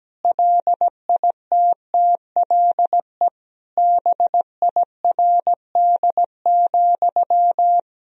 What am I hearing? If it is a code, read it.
Morse code, heard as a keyed tone, about 17 wpm, 700 Hz. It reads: LITTLE BIRD,
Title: Sentences from Top 1000 Words - Review Part 2 - 17wpm